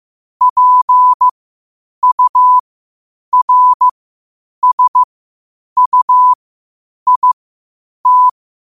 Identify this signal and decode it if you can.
Morse code, heard as a keyed tone, about 15 wpm, spaced as for 8 wpm, 1000 Hz. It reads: PURSUIT
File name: qotdfiles / 8wpm